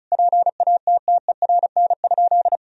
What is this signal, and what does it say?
Morse code, heard as a keyed tone, about 34 wpm, 700 Hz. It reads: PATTERN?
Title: Sentences from Top 1000 Words - Review Part 2 - 35wpm